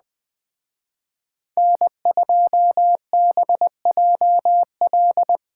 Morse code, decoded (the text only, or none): N2BJL